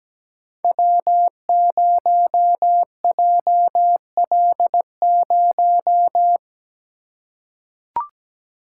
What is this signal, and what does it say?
Morse code, heard as a keyed tone, about 17 wpm, 700 Hz. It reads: W0JL0
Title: Code Groups - 5 Letter:Numbers - 17wpm